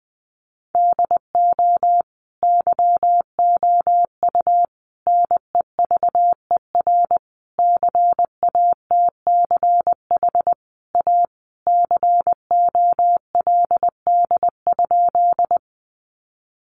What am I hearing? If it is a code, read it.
Morse code, heard as a keyed tone, about 20 wpm, 700 Hz. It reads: DO YOU NEVER CATCH A COLD?
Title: Sentences from Top 1500 Words 20wpm